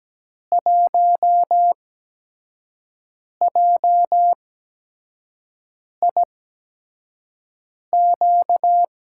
Morse code, heard as a keyed tone, about 17 wpm, 700 Hz. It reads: 1 J I Q
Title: USA Call Signs - 6 Characters - 8x Character Spacing - 17wpm